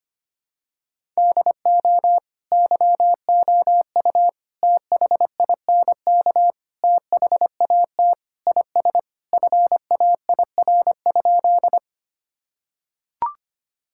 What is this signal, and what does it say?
Morse code, heard as a keyed tone, about 25 wpm, 700 Hz. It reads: DO YOU THINK THAT IS FAIR?
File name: practice / 25wpm / Sentences with 4-Word Phrases - 25wpm